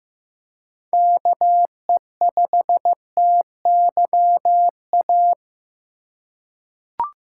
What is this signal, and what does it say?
Morse code, heard as a keyed tone, about 15 wpm, 700 Hz. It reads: KE5TYA
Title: USA Call Signs - 6 Characters - 2x Character Spacing - 15wpm